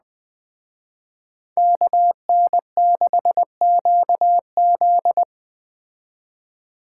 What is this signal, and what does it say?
Morse code, heard as a keyed tone, about 21 wpm, 700 Hz. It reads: KN6QZ